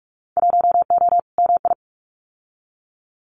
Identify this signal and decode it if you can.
Morse code, heard as a keyed tone, about 46 wpm, 700 Hz. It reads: 1O MI
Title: Winter Field Day Exchanges 45wpm